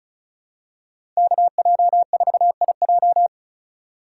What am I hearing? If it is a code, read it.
Morse code, heard as a keyed tone, about 36 wpm, 700 Hz. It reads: KJ4IJ
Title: USA Call Signs - 5 Characters - 4x Character Spacing - 35wpm